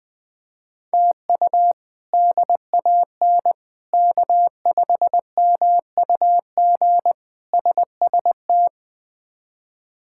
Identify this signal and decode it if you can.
Morse code, heard as a keyed tone, about 20 wpm, 700 Hz. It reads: TU DAN K5MUG SST
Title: K1USN SST QSO Practice - 20wpm